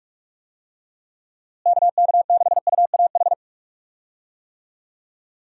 Morse code, heard as a keyed tone, about 46 wpm, 700 Hz. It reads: KK6UAH
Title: USA Call Signs - 6 Characters - Repeated 3x - 45wpm